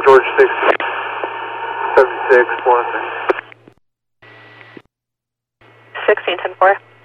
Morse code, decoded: none

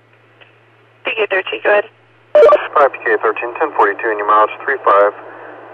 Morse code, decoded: none

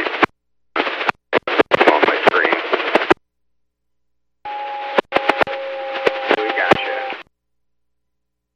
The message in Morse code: none

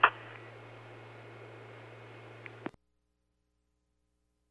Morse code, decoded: none

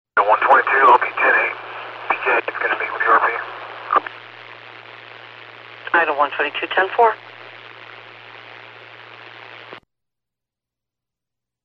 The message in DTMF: none

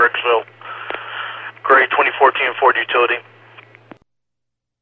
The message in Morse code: none

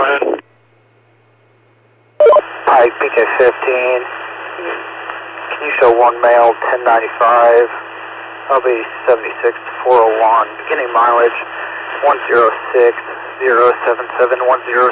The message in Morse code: none

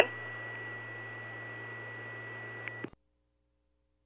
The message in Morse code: none

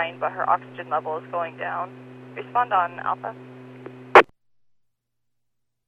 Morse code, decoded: none

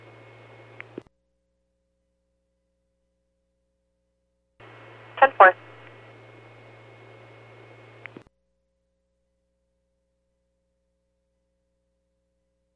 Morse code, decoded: none

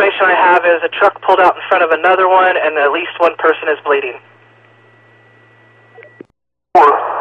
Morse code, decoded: none

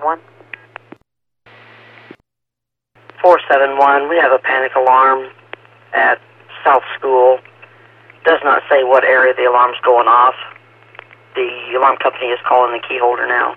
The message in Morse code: none

